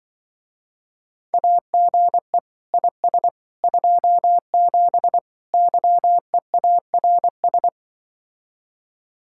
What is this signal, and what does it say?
Morse code, heard as a keyed tone, about 24 wpm, 700 Hz. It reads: AGE IS 27 YEARS
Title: Speed-Racing Common QSO Phrases - 20wpm